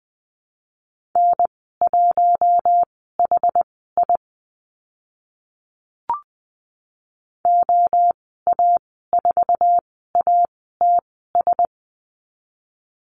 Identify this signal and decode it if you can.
Morse code, heard as a keyed tone, about 20 wpm, spaced as for 15 wpm, 700 Hz. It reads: N1HI OA4ATS
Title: 062 Lesson 11g - Letter h - Call Signs 20wpm